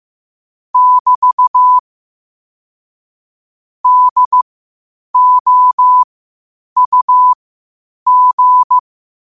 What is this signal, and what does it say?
Morse code, heard as a keyed tone, about 15 wpm, spaced as for 8 wpm, 1000 Hz. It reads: <BT> DOUG